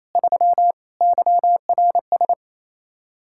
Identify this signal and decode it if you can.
Morse code, heard as a keyed tone, about 36 wpm, 700 Hz. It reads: 3 YRS